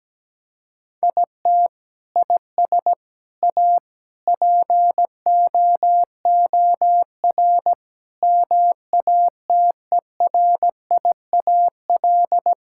Morse code, heard as a keyed tone, about 17 wpm, 700 Hz. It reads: IT IS A POOR MATERIAL
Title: Sentences from Top 1000 Words - Review Part 5 - 17wpm